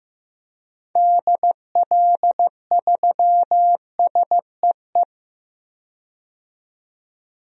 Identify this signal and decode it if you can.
Morse code, heard as a keyed tone, about 15 wpm, 700 Hz. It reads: DL3SEE
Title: DXCC-100 Call Signs - 6 Characters - 15wpm